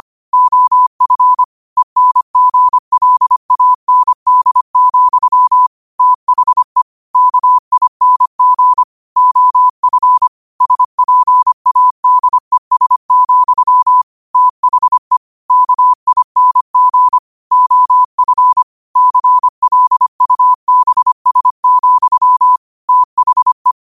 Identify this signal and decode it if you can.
Morse code, heard as a keyed tone, about 25 wpm, 1000 Hz. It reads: OF ENGLAND, THE KING OF SPADES, THE KING OF CLUBS, THE
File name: qotdfiles / 25wpm